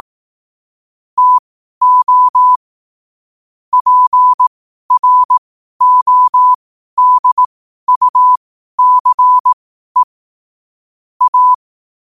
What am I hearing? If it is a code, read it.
Morse code, heard as a keyed tone, about 18 wpm, spaced as for 12 wpm, 1000 Hz. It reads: TO PRODUCE A
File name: qotdfiles / 12wpm